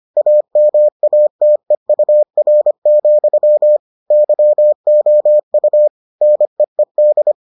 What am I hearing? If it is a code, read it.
Morse code, heard as a keyed tone, about 25 wpm, 600 Hz. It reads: AMATEUR, YOU NEED